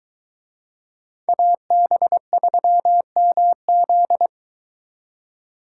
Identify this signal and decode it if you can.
Morse code, heard as a keyed tone, about 23 wpm, 700 Hz. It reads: AB3MZ